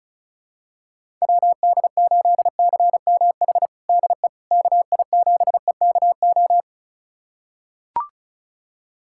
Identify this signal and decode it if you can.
Morse code, heard as a keyed tone, about 36 wpm, 700 Hz. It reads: WD8CMH DE KI7EKO
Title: USA Callsign DE Callsign 35wpm